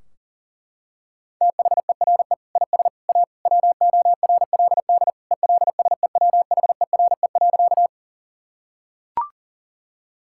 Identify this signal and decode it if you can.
Morse code, heard as a keyed tone, about 40 wpm, 700 Hz. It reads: THERE IS A WORLD ELSEWHERE.
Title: Sentences from Top 1600 Words 40wpm